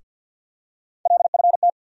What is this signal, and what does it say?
Morse code, heard as a keyed tone, about 53 wpm, 700 Hz. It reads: LFT
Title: USA Call Sign Postfixes - 3 Letters 50wpm